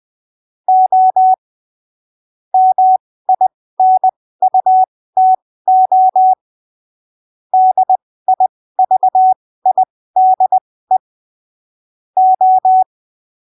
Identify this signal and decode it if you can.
Morse code, heard as a keyed tone, about 20 wpm, spaced as for 15 wpm, 750 Hz. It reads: O MINUTO DIVIDE O
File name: arquivos / texto / texto 29 20ppm